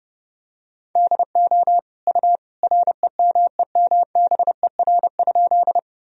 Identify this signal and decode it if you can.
Morse code, heard as a keyed tone, about 30 wpm, 700 Hz. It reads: DO U REMEMBER?
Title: Sentences from Top 400 Words and Abbreviations 30wpm